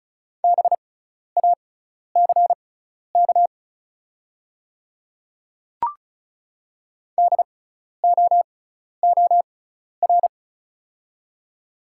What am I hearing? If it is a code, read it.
Morse code, heard as a keyed tone, about 36 wpm, spaced as for 12 wpm, 690 Hz. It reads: BACK DOOR
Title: 6x Character Spacing - Top 500 Words - 35wpm